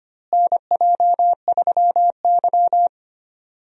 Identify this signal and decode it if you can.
Morse code, heard as a keyed tone, about 25 wpm, 700 Hz. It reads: NJ3Y
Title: USA Call Signs - 4 Characters - Repeated 3x - 25wpm